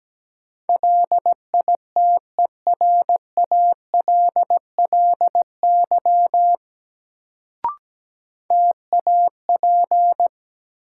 Morse code, heard as a keyed tone, about 17 wpm, 700 Hz. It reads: LITERALLY TAP
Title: Sentences from Top 1400 Words - New Word Drill - 17wpm